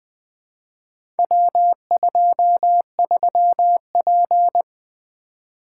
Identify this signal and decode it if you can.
Morse code, heard as a keyed tone, about 20 wpm, 700 Hz. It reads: W23P
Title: Four Letter-Number Combinations - 20wpm